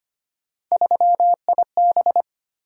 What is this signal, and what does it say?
Morse code, heard as a keyed tone, about 25 wpm, 700 Hz. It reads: 3IB
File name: Three Letter-Number Combinations - 25wpm